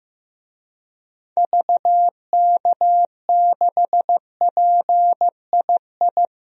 Morse code, heard as a keyed tone, about 15 wpm, 700 Hz. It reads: VK6PII
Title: DXCC-100 Call Signs - 6 Characters - 15wpm